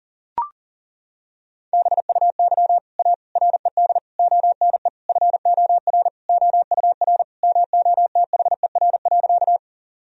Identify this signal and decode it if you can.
Morse code, heard as a keyed tone, about 40 wpm, 700 Hz. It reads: BUY A RED ONE FOR OUR MOTHER.
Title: Sentences from Top 1000 Words - Review Part 2 - 40wpm